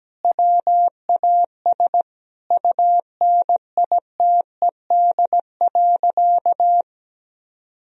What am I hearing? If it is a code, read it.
Morse code, heard as a keyed tone, about 17 wpm, 700 Hz. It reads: WAS UNITED.
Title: Sentences from Top 1000 Words - Review Part 4 - 17wpm